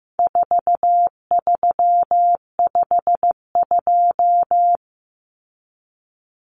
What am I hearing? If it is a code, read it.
Morse code, heard as a keyed tone, about 15 wpm, 700 Hz. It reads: 4352
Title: Common Cut Numbers - 4-Digit 15wpm